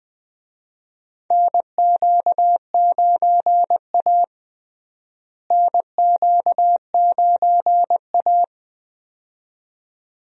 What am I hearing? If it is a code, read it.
Morse code, heard as a keyed tone, about 20 wpm, 700 Hz. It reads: NQ9A NQ9A
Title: USA Call Signs - 4 Characters - Repeated 2x - 20wpm